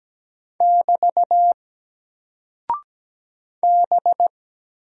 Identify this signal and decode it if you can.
Morse code, heard as a keyed tone, about 17 wpm, 700 Hz. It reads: <BT> B
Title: Single Character (Letter:Number:Punctuation:Prosign) Alt 17wpm